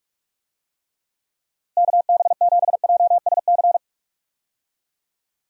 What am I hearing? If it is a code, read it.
Morse code, heard as a keyed tone, about 46 wpm, 700 Hz. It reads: KB7JSC
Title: USA Call Signs - 6 Characters - Repeated 2x - 45wpm